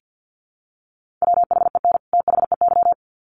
Morse code, heard as a keyed tone, about 50 wpm, 700 Hz. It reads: WHEN THEY
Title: Sets of 2 Words 50wpm